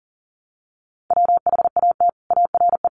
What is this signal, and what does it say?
Morse code, heard as a keyed tone, about 37 wpm, 700 Hz. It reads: WHAT ARE
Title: Sets of 5 Words Encore 40wpm